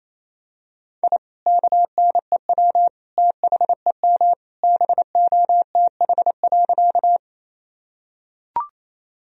Morse code, heard as a keyed tone, about 28 wpm, 700 Hz. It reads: I KNEW THEM BOTH.
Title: Sentences from Top 1000 Words - Review Part 1 - 28wpm